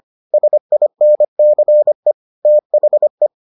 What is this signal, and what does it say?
Morse code, heard as a keyed tone, about 25 wpm, 600 Hz. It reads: SINCE THE